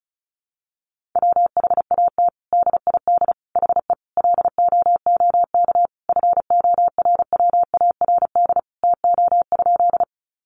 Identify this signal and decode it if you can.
Morse code, heard as a keyed tone, about 34 wpm, 700 Hz. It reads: WHAT DID HE LOOK FORWARD TO?
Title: Sentences from Top 1300 Words 35wpm